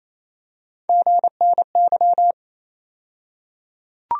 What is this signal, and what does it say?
Morse code, heard as a keyed tone, about 28 wpm, 700 Hz. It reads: GNY
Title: USA Call Sign Postfixes - 3 Letters 28wpm